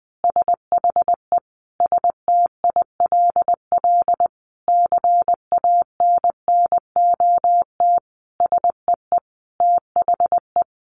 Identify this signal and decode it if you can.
Morse code, heard as a keyed tone, about 19 wpm, 700 Hz. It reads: SHE STILL CANNOT SEE THE
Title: Additional Easy Phrases 20wpm